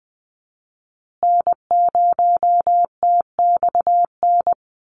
Morse code, heard as a keyed tone, about 20 wpm, 700 Hz. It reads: N0TXN
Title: USA Call Signs Repeated 2x 20wpm